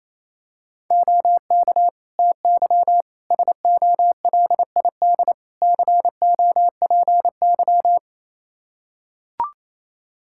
Morse code, heard as a keyed tone, about 28 wpm, 700 Hz. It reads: OK TY SOLID COPY
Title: K1USN SST QSO Practice 28wpm